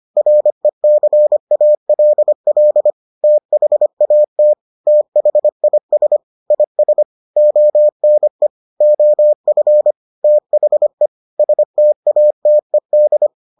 Morse code, heard as a keyed tone, about 25 wpm, 600 Hz. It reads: RECALL THAT THIS IS ONE OF THE STATED